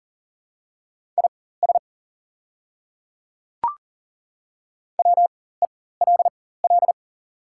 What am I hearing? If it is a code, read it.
Morse code, heard as a keyed tone, about 39 wpm, spaced as for 19 wpm, 700 Hz. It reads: IS WELL
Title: 4x Character Spacing - Top 200 Words - 40wpm